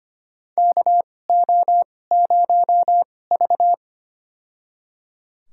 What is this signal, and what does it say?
Morse code, heard as a keyed tone, about 25 wpm, spaced as for 18 wpm, 700 Hz. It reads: KO0V
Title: USA Call Signs - 4 Characters - 2x Character Spacing - 25wpm